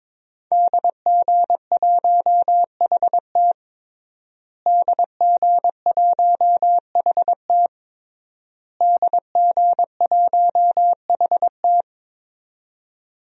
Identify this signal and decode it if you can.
Morse code, heard as a keyed tone, about 22 wpm, 700 Hz. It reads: DG1HT DG1HT DG1HT
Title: DXCC-100 Call Signs - 5 Characters - Repeated 3x - 22wpm